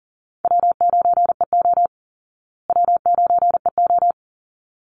Morse code, heard as a keyed tone, about 40 wpm, 700 Hz. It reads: W9EO W9EO
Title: Call Signs Repeated 3x - 40wpm